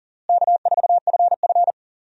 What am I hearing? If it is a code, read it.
Morse code, heard as a keyed tone, about 37 wpm, 700 Hz. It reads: K4FF
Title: USA Call Signs - 4 Characters - Repeated 3x - 40wpm